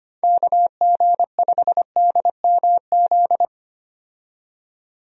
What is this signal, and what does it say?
Morse code, heard as a keyed tone, about 25 wpm, 700 Hz. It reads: KG5DMZ
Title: USA Call Signs - 6 Characters - 25wpm